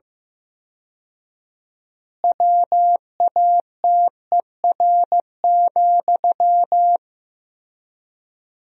Morse code, heard as a keyed tone, about 15 wpm, 700 Hz. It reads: WATER,
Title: 4x Word-Spacing - Sentences from Top 400 Words - 15wpm